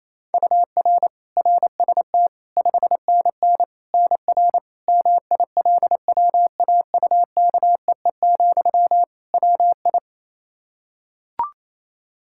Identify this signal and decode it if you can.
Morse code, heard as a keyed tone, about 28 wpm, 700 Hz. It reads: UR RST 5NN NR MILWAUKEE, WI
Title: Practice QSO - Exchange 1 - Beginning - 28wpm